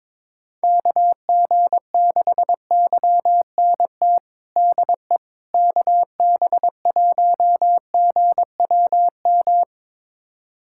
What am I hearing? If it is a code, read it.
Morse code, heard as a keyed tone, about 22 wpm, 700 Hz. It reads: KG6YNT DE KB1GWM
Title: USA Callsign DE Callsign - 6 Characters - 22wpm